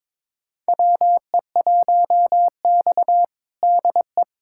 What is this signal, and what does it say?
Morse code, heard as a keyed tone, about 22 wpm, 700 Hz. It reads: WE1X DE
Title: USA Callsign DE Callsign - 4 Characters - 22wpm